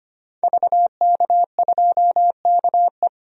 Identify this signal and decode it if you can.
Morse code, heard as a keyed tone, about 25 wpm, 700 Hz. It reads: VK2KE
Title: DXCC-100 Call Signs - 5 Characters - 25wpm